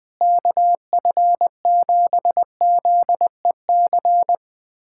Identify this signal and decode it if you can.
Morse code, heard as a keyed tone, about 20 wpm, 700 Hz. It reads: KF7ZEC
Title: USA Callsign DE Callsign - 6 Characters - 20wpm